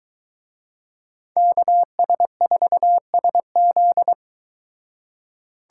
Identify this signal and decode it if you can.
Morse code, heard as a keyed tone, about 23 wpm, 700 Hz. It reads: KS4SZ